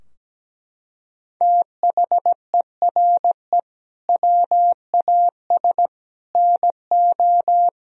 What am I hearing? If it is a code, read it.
Morse code, heard as a keyed tone, about 17 wpm, 700 Hz. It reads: THERE WAS NO
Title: Sentences from Top 1000 Words - Review Part 3 - 17wpm